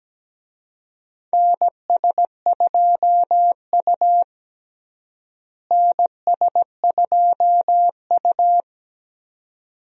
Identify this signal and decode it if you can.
Morse code, heard as a keyed tone, about 17 wpm, 700 Hz. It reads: NS2U NS2U
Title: USA Call Signs - 4 Characters - Repeated 2x - 17wpm